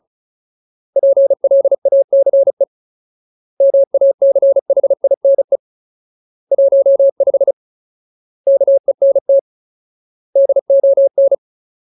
Morse code, heard as a keyed tone, about 37 wpm, 550 Hz. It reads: PLACE MACHINE 15 KENT DON
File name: Ultimate Word List / 35WPM Ultimate